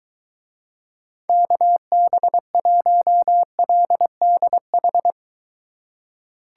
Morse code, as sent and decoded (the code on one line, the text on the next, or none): -.- -... .---- .-.. -.. ....
KB1LDH